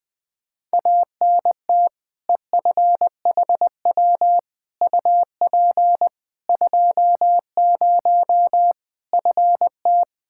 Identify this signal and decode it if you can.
Morse code, heard as a keyed tone, about 20 wpm, 700 Hz. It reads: ANT EFHW UP 20 FT
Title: Practice QSO - Exchange 2 - Info - 20wpm